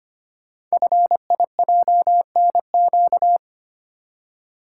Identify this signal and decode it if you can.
Morse code, heard as a keyed tone, about 25 wpm, 700 Hz. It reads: FIJNQ
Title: Code Groups - 5 Letters - 25wpm